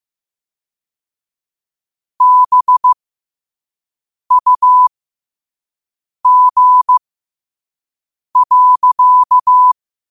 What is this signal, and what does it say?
Morse code, heard as a keyed tone, about 15 wpm, spaced as for 5 wpm, 1000 Hz. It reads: BUG.
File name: qotdfiles / 5wpm